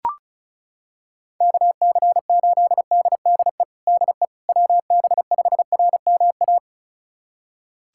Morse code, heard as a keyed tone, about 34 wpm, 690 Hz. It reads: KC8DDE DE WB5RMA
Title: USA Callsign DE Callsign - 6 Characters - 35wpm